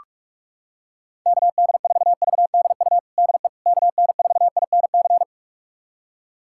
Morse code, heard as a keyed tone, about 41 wpm, 700 Hz. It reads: KD4VDU DE KN4INC
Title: USA Callsign DE Callsign - 6 Characters - 45wpm